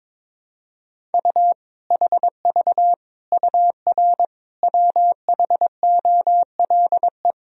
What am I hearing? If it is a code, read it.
Morse code, heard as a keyed tone, about 22 wpm, 700 Hz. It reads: U HV UR WHOLE
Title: Sentences from Top 400 Words and Abbreviations 22wpm